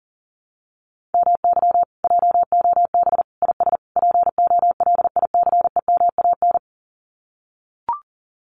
Morse code, heard as a keyed tone, about 40 wpm, 700 Hz. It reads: MY JOB IS POLICEMAN
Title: Common QSO Phrases Encore 40wpm